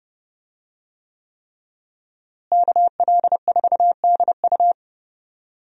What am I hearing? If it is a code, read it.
Morse code, heard as a keyed tone, about 30 wpm, 700 Hz. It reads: KL4DU
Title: USA Call Signs - 5 Characters - Repeated 3x - 30wpm